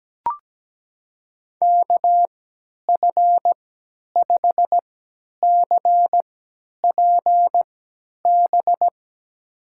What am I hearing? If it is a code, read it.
Morse code, heard as a keyed tone, about 17 wpm, spaced as for 10 wpm, 700 Hz. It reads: KF5CPB
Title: USA Call Signs - 6 Characters - 3x Character Spacing - 17wpm